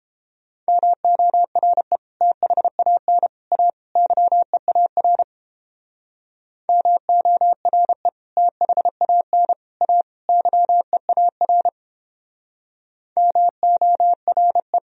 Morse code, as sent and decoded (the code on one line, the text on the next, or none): -- --- .-. / - .... .- -. / .- / -.-- . .- .-. / -- --- .-. / - .... .- -. / .- / -.-- . .- .-. / -- --- .-. .
MORE THAN A YEAR MORE THAN A YEAR MORE